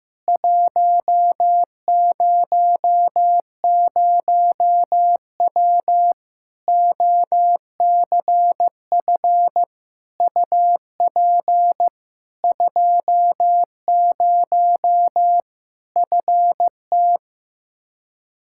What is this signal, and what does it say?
Morse code, heard as a keyed tone, about 15 wpm, 700 Hz. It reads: 100W OCF UP 20 FT
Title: Practice QSO - Exchange 2 - Info - 15wpm